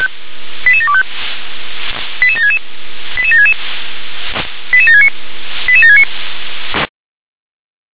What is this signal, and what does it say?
Five-tone selective call (ZVEI), heard as quick runs of five tones, tones of about 70 ms. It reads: 88725 88766 88766 88769 88769